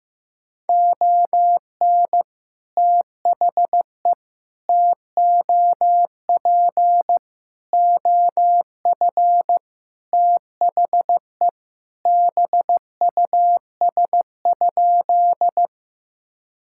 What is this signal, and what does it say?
Morse code, heard as a keyed tone, about 15 wpm, 700 Hz. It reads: ON THE TOP OF THE BUS?
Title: Sentences from Top 1000 Words - Review Part 6 - 15wpm